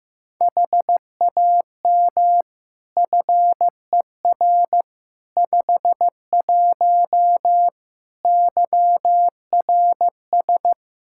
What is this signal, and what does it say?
Morse code, heard as a keyed tone, about 15 wpm, 700 Hz. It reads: HAM FER 51 YRS
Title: Practice QSO - Exchange 3 - More Info - 15wpm